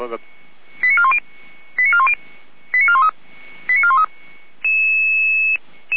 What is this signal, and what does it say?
Five-tone selective call (ZVEI), heard as quick runs of five tones, tones of about 70 ms, with speech. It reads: 89410 89410 89413 89413